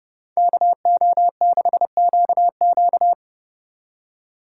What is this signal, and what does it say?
Morse code, heard as a keyed tone, about 39 wpm, 700 Hz. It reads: KO6QQ